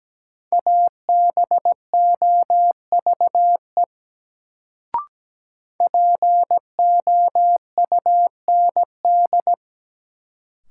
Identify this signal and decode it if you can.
Morse code, heard as a keyed tone, about 17 wpm, 700 Hz. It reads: ABOVE POUND
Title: Sentences from Top 1000 Words - Review Part 2 - New Word Drill - 17wpm